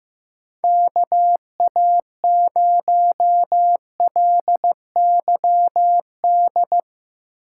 Morse code, heard as a keyed tone, about 15 wpm, 700 Hz. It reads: KA0LYD
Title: USA Call Signs - 6 Characters - 4x Character Spacing - 15wpm